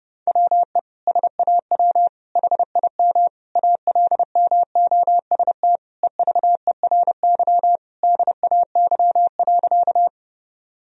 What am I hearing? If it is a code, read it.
Morse code, heard as a keyed tone, about 29 wpm, 700 Hz. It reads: WE SAW HIM ALMOST EVERY DAY.
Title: Sentences from Top 1000 Words - Review Part 4 - 30wpm